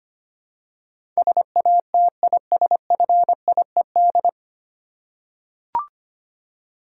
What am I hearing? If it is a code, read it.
Morse code, heard as a keyed tone, about 25 wpm, 700 Hz. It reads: SATISFIED
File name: practice / 25wpm / 9 Letter Words - First 3 Letters Given - 25wpm